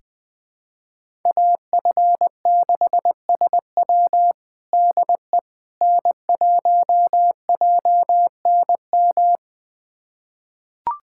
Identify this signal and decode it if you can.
Morse code, heard as a keyed tone, about 20 wpm, 700 Hz. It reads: AF6SW DE N1JNM
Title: USA Callsign DE Callsign - 5 Characters - 20wpm